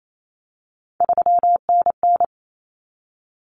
Morse code, heard as a keyed tone, about 28 wpm, 700 Hz. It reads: 3NN
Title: Common Cut Numbers - 3-Digit 28wpm